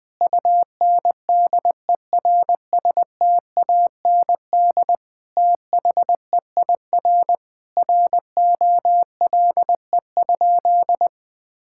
Morse code, heard as a keyed tone, about 21 wpm, 700 Hz. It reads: UNDERSTAND THEIR ROLE?
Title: Sentences from Top 1000 Words - Review Part 6 - 20wpm